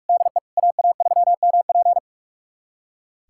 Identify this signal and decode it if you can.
Morse code, heard as a keyed tone, about 41 wpm, 700 Hz. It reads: DE AA3MP
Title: USA Callsign DE Callsign - 5 Characters - 45wpm